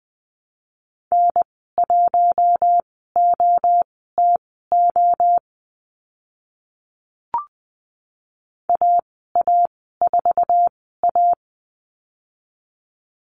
Morse code, heard as a keyed tone, about 20 wpm, spaced as for 14 wpm, 700 Hz. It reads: N1OTO AA4A
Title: 069 Lesson 12g - Letter d - Call Signs 20wpm